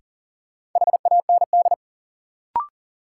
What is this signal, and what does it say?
Morse code, heard as a keyed tone, about 40 wpm, 700 Hz. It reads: HAND
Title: Sentences from Top 200 Words - New Word Drill - 40wpm